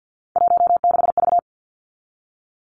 Morse code, heard as a keyed tone, about 50 wpm, 700 Hz. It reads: 16V
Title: Three Challenging Letter-Number Combinations 50wpm